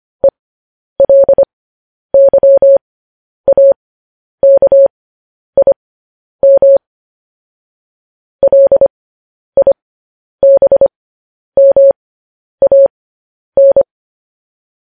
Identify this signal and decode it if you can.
Morse code, heard as a keyed tone, about 25 wpm, spaced as for 10 wpm, 550 Hz. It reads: ELYAKIM LIBMAN